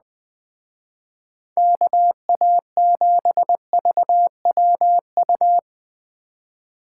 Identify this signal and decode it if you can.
Morse code, heard as a keyed tone, about 19 wpm, 700 Hz. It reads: KA7VWU
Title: Speed Racing - USA 2x3 Call Signs 17wpm